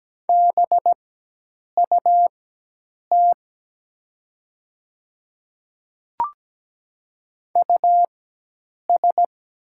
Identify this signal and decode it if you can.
Morse code, heard as a keyed tone, about 17 wpm, spaced as for 8 wpm, 700 Hz. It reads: BUT US
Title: 4x Character Spacing - Top 200 Words - 17wpm